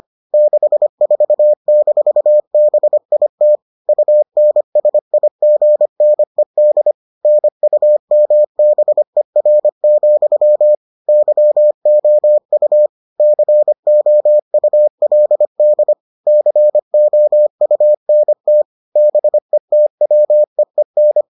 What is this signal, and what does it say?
Morse code, heard as a keyed tone, about 25 wpm, 600 Hz. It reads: 64-BIT UNSIGNED NUMBER, YOU COULD COUNT BETWEEN